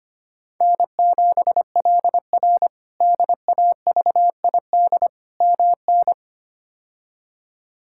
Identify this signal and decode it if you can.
Morse code, heard as a keyed tone, about 25 wpm, 700 Hz. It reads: N7LR DAVID MN
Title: K1USN SST QSO Practice - 25wpm